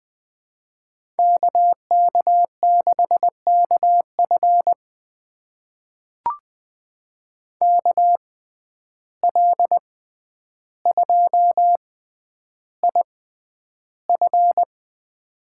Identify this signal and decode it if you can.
Morse code, heard as a keyed tone, about 20 wpm, spaced as for 10 wpm, 700 Hz. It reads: KK6KF KL2IF
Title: USA Call Signs - 5 Characters - 6x Character Spacing - 20wpm